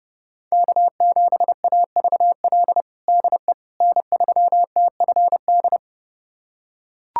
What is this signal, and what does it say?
Morse code, heard as a keyed tone, about 31 wpm, 700 Hz. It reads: K7AVL DE N3TFD
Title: USA Callsign DE Callsign - 5 Characters - 30wpm